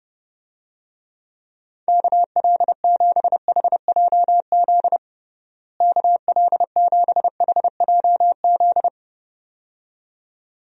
Morse code, heard as a keyed tone, about 30 wpm, 700 Hz. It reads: KL7HJZ KL7HJZ
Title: DXCC-100 Call Signs - 6 Characters - Repeated 2x - 30wpm